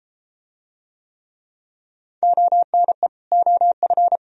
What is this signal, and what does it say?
Morse code, heard as a keyed tone, about 32 wpm, 700 Hz. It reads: ONE OF